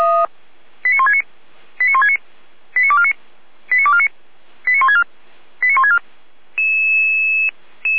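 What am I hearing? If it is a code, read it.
Five-tone selective call (ZVEI), heard as quick runs of five tones, tones of about 70 ms. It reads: 89169 89169 89249 89249 89164 89164
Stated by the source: ZVEI five-tone selective call received by SDR, recorded on the Feuerwehr Orlamünde audio server.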